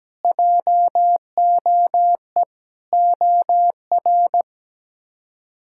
Morse code, heard as a keyed tone, about 17 wpm, 700 Hz. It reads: JOE OR
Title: K1USN SST QSO Practice - 17wpm